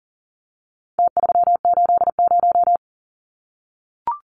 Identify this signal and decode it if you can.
Morse code, heard as a keyed tone, about 40 wpm, 700 Hz. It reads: T380
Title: Numbers - Contest Style - 4-Digit 40wpm